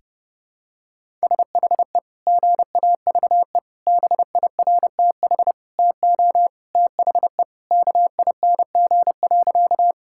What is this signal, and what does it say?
Morse code, heard as a keyed tone, about 28 wpm, 700 Hz. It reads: SHE GAVE BIRTH TO THE KING.
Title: Sentences from Top 1000 Words - Review Part 5 - 30wpm